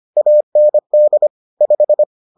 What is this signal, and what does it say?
Morse code, heard as a keyed tone, about 25 wpm, 600 Hz. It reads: AND 5